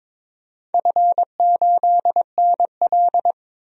Morse code, heard as a keyed tone, about 22 wpm, 700 Hz. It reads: F8NL
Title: DXCC-100 Call Signs - 4 Characters - 22wpm